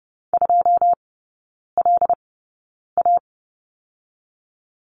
Morse code, heard as a keyed tone, about 30 wpm, spaced as for 9 wpm, 700 Hz. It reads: 2LA
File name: Lesson 19e - Letter w - Three Characters 30wpm